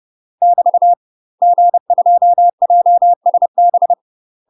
Morse code, heard as a keyed tone, about 30 wpm, 700 Hz. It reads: <BT> G2JSB
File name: UK MASTER QSOs TRAINING/30WPM